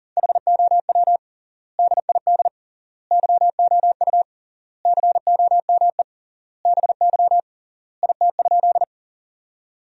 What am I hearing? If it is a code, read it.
Morse code, heard as a keyed tone, about 40 wpm, 700 Hz. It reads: HOW DID YOU COME BY IT?
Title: 3x Word-Spacing - Sentences from Top 100 Words - 40wpm